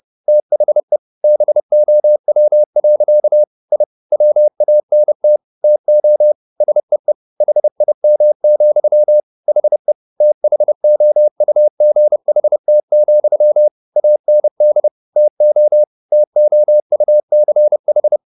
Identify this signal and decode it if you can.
Morse code, heard as a keyed tone, about 30 wpm, 600 Hz. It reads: THE BOW. I WANT TO SEE HIM, HE THOUGHT, AND TO TOUCH